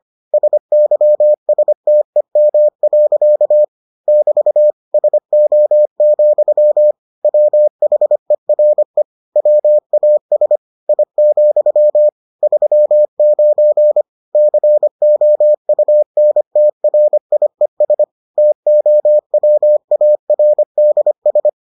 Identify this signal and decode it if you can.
Morse code, heard as a keyed tone, about 25 wpm, 600 Hz. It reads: SYSTEM. <BT> SO, WHERE WAS I, 39 COUNTRIES TOWARDS